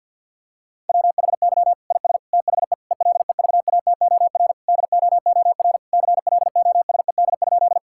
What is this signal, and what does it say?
Morse code, heard as a keyed tone, about 50 wpm, 700 Hz. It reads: WHY IS THE ELEVATOR DOOR CLOSED?
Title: Sentences from Top 1600 Words 50wpm